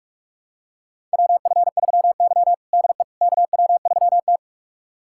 Morse code, heard as a keyed tone, about 42 wpm, 700 Hz. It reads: WF3Y DE KW3T
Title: USA Callsign DE Callsign - 4 Characters - 45wpm